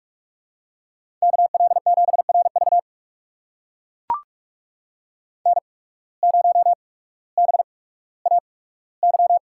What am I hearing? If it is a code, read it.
Morse code, heard as a keyed tone, about 44 wpm, spaced as for 19 wpm, 700 Hz. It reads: KL7RV N0BAY
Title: USA Call Signs - 5 Characters - 8x Character Spacing - 45wpm